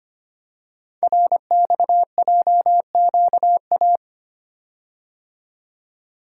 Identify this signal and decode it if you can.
Morse code, heard as a keyed tone, about 25 wpm, 700 Hz. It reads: RXJQA
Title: Code Groups - 5 Letters - 25wpm